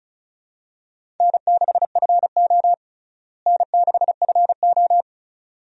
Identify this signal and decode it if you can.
Morse code, heard as a keyed tone, about 36 wpm, 700 Hz. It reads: N6FO N6FO
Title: USA Call Signs - 4 Characters - Repeated 2x - 35wpm